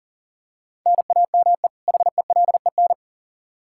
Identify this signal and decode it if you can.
Morse code, heard as a keyed tone, about 40 wpm, 700 Hz. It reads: NAME HELEN
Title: Practice QSO - Exchange 1 - Beginning - 40wpm